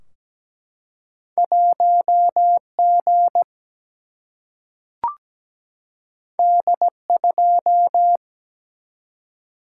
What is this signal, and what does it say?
Morse code, heard as a keyed tone, about 17 wpm, 700 Hz. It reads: 1 G D 2
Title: Single-Letter Single-Number 17wpm